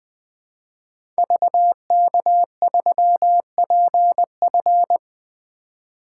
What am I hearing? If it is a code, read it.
Morse code, heard as a keyed tone, about 20 wpm, 700 Hz. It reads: VK3PF
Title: 2023 SOTA Activator Call Signs - 20wpm